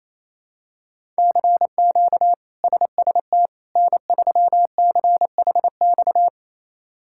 Morse code, heard as a keyed tone, about 28 wpm, 700 Hz. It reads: CQ SST N3CHX
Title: K1USN SST QSO Practice - 28wpm